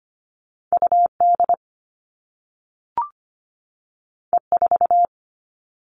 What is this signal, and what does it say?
Morse code, heard as a keyed tone, about 25 wpm, 700 Hz. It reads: UD E4
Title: All Cut Numbers - 2-Digit 25wpm